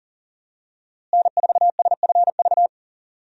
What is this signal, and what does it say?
Morse code, heard as a keyed tone, about 40 wpm, 700 Hz. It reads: N4SFV
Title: USA Call Signs - 5 Characters - Repeated 2x - 40wpm